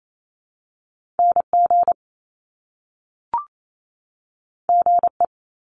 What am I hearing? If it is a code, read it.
Morse code, heard as a keyed tone, about 28 wpm, 700 Hz. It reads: NG GE
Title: Top Two Letter Combinations in Words 28wpm